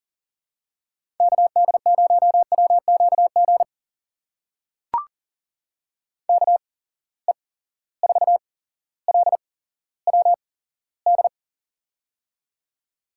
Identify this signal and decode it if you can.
Morse code, heard as a keyed tone, about 39 wpm, spaced as for 17 wpm, 700 Hz. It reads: KD0WQG KE4LWD
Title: USA Call Signs - 6 Characters - 8x Character Spacing - 40wpm